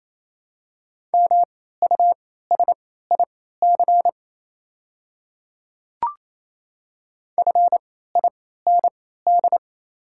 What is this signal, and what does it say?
Morse code, heard as a keyed tone, about 27 wpm, spaced as for 16 wpm, 700 Hz. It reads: MUSIC FIND
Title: 3x Character Spacing - Top 300 Words - 28wpm